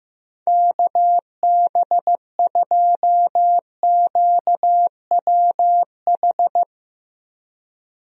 Morse code, heard as a keyed tone, about 15 wpm, 700 Hz. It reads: KB2QWH
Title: USA Call Signs - 6 Characters - Repeated 3x - 15wpm